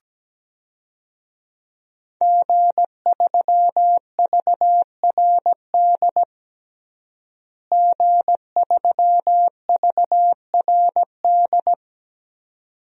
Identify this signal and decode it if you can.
Morse code, heard as a keyed tone, about 17 wpm, 700 Hz. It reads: G3VRD G3VRD
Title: DXCC-100 Call Signs - 5 Characters - Repeated 2x - 17wpm